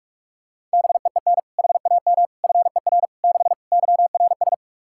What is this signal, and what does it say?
Morse code, heard as a keyed tone, about 43 wpm, 700 Hz. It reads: BEEN HAM FER 6 YRS
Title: Practice QSO - Exchange 3 - More Info - 45wpm